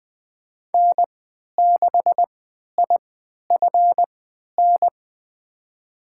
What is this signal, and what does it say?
Morse code, heard as a keyed tone, about 20 wpm, spaced as for 11 wpm, 700 Hz. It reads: N6IFN
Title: USA Call Signs - 3x Character Spacing - 20wpm